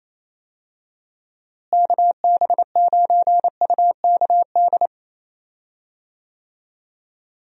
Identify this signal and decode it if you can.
Morse code, heard as a keyed tone, about 28 wpm, 700 Hz. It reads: KB9UKD
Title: USA Call Signs - 6 Characters - Repeated 3x - 28wpm